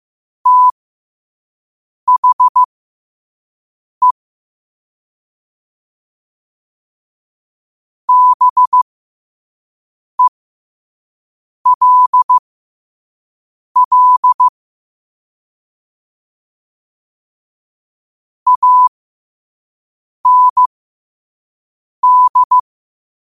Morse code, heard as a keyed tone, about 15 wpm, spaced as for 5 wpm, 1000 Hz. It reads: THE BELL AND
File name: qotdfiles / 5wpm